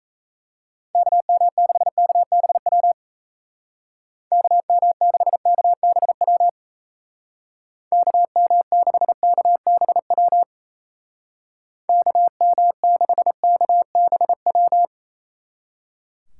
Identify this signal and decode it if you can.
Morse code, heard as a keyed tone, about 35 wpm, 700 Hz. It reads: KM6KBW KM6KBW KM6KBW KM6KBW
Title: Speed Racing - USA Call Signs 28wpm